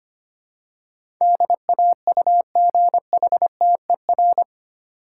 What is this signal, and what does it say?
Morse code, heard as a keyed tone, about 25 wpm, 700 Hz. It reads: DAUGHTER
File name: 8 Letter Words - First 3 Letters Given - 25wpm